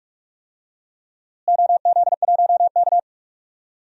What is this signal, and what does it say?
Morse code, heard as a keyed tone, about 46 wpm, 700 Hz. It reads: OZ1K
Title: DXCC-100 Call Signs - 4 Characters - Repeated 2x - 45wpm